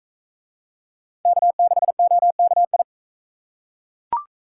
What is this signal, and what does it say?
Morse code, heard as a keyed tone, about 42 wpm, 700 Hz. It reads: K6OKI